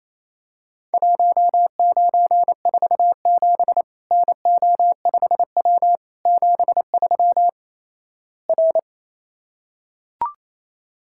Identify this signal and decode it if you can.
Morse code, heard as a keyed tone, about 28 wpm, 700 Hz. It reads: 1947 NO5W 73 R
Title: Worked All Europe DX Contest - QTC Exchange - 28wpm